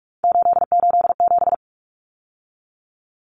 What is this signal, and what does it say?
Morse code, heard as a keyed tone, about 44 wpm, 700 Hz. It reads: 887
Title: Common Cut Numbers - 3-Digit 45wpm